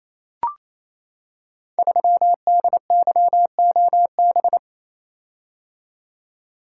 Morse code, heard as a keyed tone, about 28 wpm, 700 Hz. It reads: 3DYOB
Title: Code Groups - 5 Letter:Numbers - 28wpm